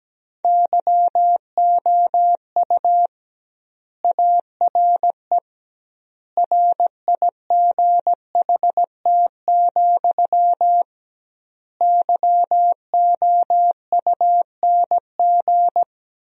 Morse code, heard as a keyed tone, about 17 wpm, 700 Hz. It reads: YOU ARE RIGHT, YOUNG
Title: 2x Word-Spacing - Sentences from Top 400 Words - 17wpm